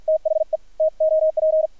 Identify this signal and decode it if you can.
Morse code, heard as a keyed tone, about 50 wpm, 630 Hz, with radio noise. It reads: THE TOP
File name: archived / CRSnet